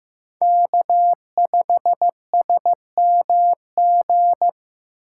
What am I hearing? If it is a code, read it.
Morse code, heard as a keyed tone, about 15 wpm, 700 Hz. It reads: K5SMG
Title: USA Call Signs - 3x Character Spacing - 15wpm